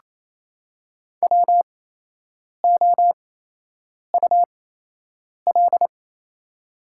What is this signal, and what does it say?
Morse code, heard as a keyed tone, about 28 wpm, spaced as for 8 wpm, 700 Hz. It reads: WOUL
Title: 8x Character Spacing - Top 100 Words - 28wpm